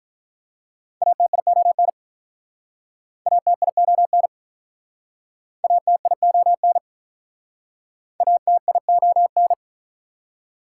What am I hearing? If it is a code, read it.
Morse code, heard as a keyed tone, about 48 wpm, 690 Hz. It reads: ATION ATION ATION ATION